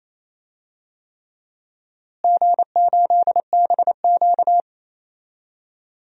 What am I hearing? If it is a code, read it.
Morse code, heard as a keyed tone, about 28 wpm, 700 Hz. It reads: G8BQ
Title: DXCC-100 Call Signs - 4 Characters - Repeated 3x - 28wpm